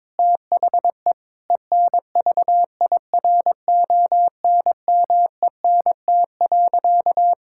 Morse code, heard as a keyed tone, about 21 wpm, 700 Hz. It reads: THE ENVIRONMENT.
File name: Difficult Sentences with Hints - 22wpm